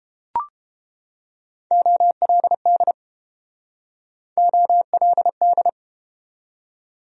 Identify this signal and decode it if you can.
Morse code, heard as a keyed tone, about 29 wpm, 700 Hz. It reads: OLD OLD